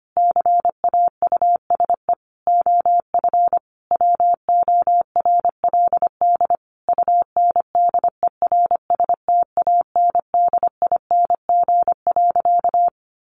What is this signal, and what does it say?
Morse code, heard as a keyed tone, about 25 wpm, 700 Hz. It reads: CAUSE OF WORLD UNDERSTANDING.
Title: Sentences from Top 1400 Words 25wpm